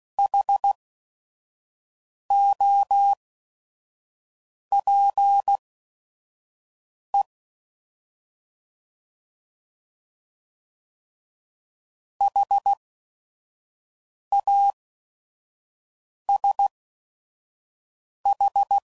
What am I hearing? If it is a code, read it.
Morse code, heard as a keyed tone, about 16 wpm, spaced as for 5 wpm, 800 Hz. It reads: HOPE HASH
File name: Prefixes / By First Letter / Hs